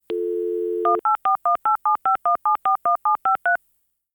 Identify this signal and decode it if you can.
DTMF, heard as telephone touch tones, tones of about 120 ms. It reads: 18418751741753
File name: modem-dialing